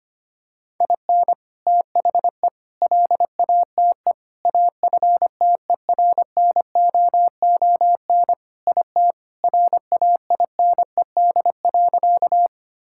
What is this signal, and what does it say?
Morse code, heard as a keyed tone, about 25 wpm, 700 Hz. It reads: IN THE LATE AFTERNOON IT RAINED.
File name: practice / 25wpm / Sentences with 3-Word Phrases - 25wpm